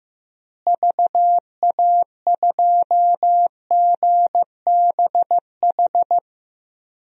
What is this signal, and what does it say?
Morse code, heard as a keyed tone, about 16 wpm, 700 Hz. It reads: VA2GBH
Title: DXCC-100 Call Signs - 6 Characters - Repeated 3x - 15wpm